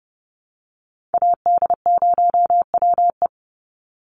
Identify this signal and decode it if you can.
Morse code, heard as a keyed tone, about 29 wpm, 700 Hz. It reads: AD0WE
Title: Call Signs - 30wpm